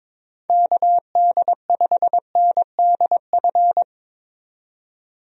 Morse code, heard as a keyed tone, about 22 wpm, 700 Hz. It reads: KD5NDF
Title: USA Call Signs - 6 Characters - 22wpm